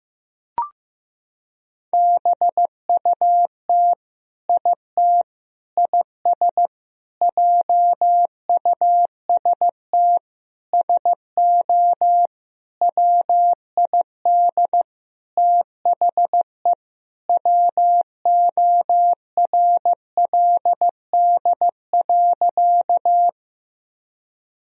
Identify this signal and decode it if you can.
Morse code, heard as a keyed tone, about 15 wpm, 700 Hz. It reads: BUT IT IS JUST SO WID THE WORLD.
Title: Sentences from Top 200 Words and Abbreviations 15wpm